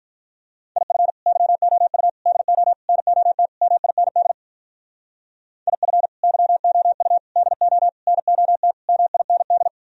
Morse code, heard as a keyed tone, about 44 wpm, 700 Hz. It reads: IF YOU DO NOT MIND IF YOU DO NOT MIND
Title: Speed-Racing - Sets of 5 Words 35wpm